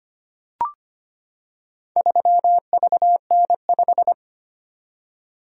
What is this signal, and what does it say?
Morse code, heard as a keyed tone, about 25 wpm, 700 Hz. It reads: 3VN5
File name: practice / 25wpm / Four Letter-Number Combinations - 25wpm